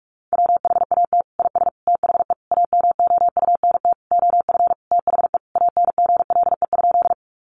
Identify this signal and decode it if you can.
Morse code, heard as a keyed tone, about 44 wpm, 700 Hz. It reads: WHAT IS THE AMOUNT OF THE ANGLE?
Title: Sentences from Top 1500 Words 45wpm